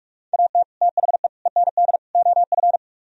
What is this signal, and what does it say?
Morse code, heard as a keyed tone, about 47 wpm, 700 Hz. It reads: AT THE END OF